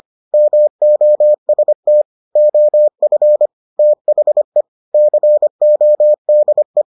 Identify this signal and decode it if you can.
Morse code, heard as a keyed tone, about 25 wpm, 600 Hz. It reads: MOST OF THE CODE